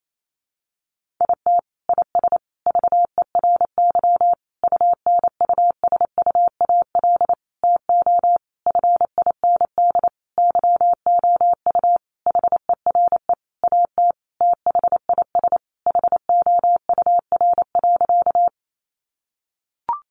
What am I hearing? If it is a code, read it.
Morse code, heard as a keyed tone, about 28 wpm, 700 Hz. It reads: IT IS VERY UNUSUAL TO FIND YOU HERE AT THIS HOUR.
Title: Sentences from Top 1400 Words 28wpm